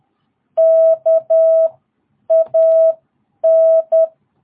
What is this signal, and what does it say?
Morse code, heard as a keyed tone, about 10 wpm, spaced as for 8 wpm, 660 Hz. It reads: KAN
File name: jh8cbh / 2022cw